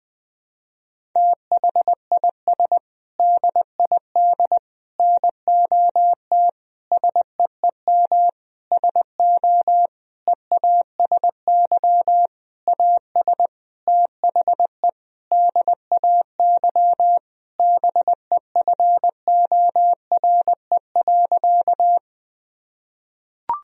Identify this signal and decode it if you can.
Morse code, heard as a keyed tone, about 20 wpm, 700 Hz. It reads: THIS DID NOT SEEM SO EASY AS THE DAY BEFORE.
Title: Sentences from Top 1000 Words - Review Part 1 - 20wpm